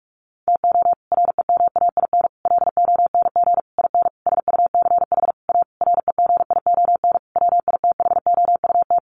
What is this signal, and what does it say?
Morse code, heard as a keyed tone, about 44 wpm, 700 Hz. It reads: TO REMAIN LONG IN SUCH A REGION WITHOUT
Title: Sentences from Top 1500 Words 45wpm